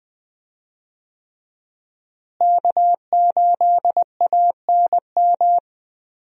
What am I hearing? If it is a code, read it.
Morse code, heard as a keyed tone, about 20 wpm, 700 Hz. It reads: K8ANM
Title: USA Call Signs - 5 Characters - Repeated 3x - 20wpm